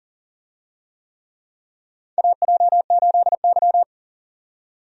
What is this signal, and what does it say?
Morse code, heard as a keyed tone, about 40 wpm, 700 Hz. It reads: AJ8Y
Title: USA Call Signs - 4 Characters - 40wpm